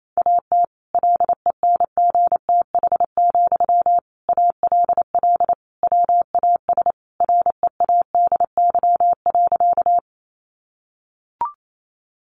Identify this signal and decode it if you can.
Morse code, heard as a keyed tone, about 28 wpm, 700 Hz. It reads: AT LENGTH, ALL WAS READY.
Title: Sentences from Top 1000 Words 28wpm